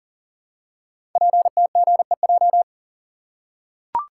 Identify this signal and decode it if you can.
Morse code, heard as a keyed tone, about 40 wpm, 690 Hz. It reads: PTGEJ
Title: Code Groups - 5 Letters - 40wpm